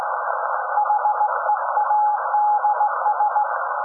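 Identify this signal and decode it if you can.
Morse code, heard as a keyed tone, about 18 wpm, 840 Hz, with noise. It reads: HEADS
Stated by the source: SAQ Grimeton Radio, Alexanderson 200 kW alternator on 17.2 kHz, received in Potsdam with a very strong signal on World Radio Day.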